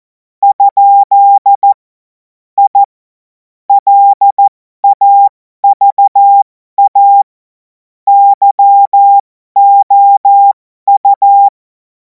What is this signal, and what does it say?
Morse code, heard as a keyed tone, about 14 wpm, spaced as for 12 wpm, 800 Hz. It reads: ? I LAVA YOU
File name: Morsekurs / Witze